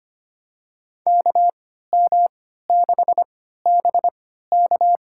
Morse code, heard as a keyed tone, about 25 wpm, spaced as for 14 wpm, 700 Hz. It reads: KM6BK
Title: USA Call Signs - 3x Character Spacing - 25wpm